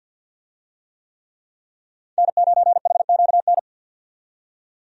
Morse code, heard as a keyed tone, about 50 wpm, 700 Hz. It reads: N9HQN
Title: USA Call Signs - 5 Characters - 50wpm